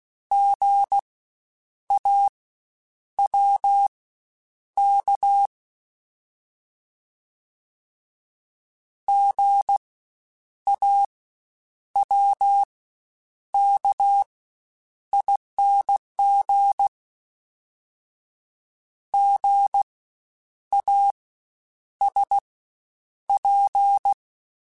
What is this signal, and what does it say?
Morse code, heard as a keyed tone, about 16 wpm, spaced as for 8 wpm, 800 Hz. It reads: GAWK GAWKING GASP
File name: Spaced Alphas / Spaced Alphas mp3s / Gs sp @15